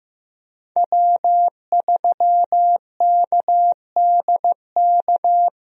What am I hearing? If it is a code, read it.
Morse code, heard as a keyed tone, about 15 wpm, 700 Hz. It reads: W3KDK
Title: USA Call Signs - 5 Characters - 15wpm